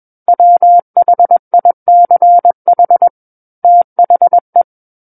Morse code, heard as a keyed tone, about 21 wpm, 700 Hz. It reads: WHICH THE
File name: Book One / 12-Chapter 12 - 21 WPM